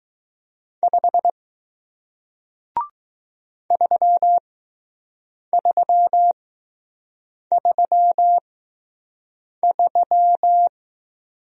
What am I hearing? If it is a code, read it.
Morse code, heard as a keyed tone, about 20 wpm, spaced as for 6 wpm, 700 Hz. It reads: 5 3333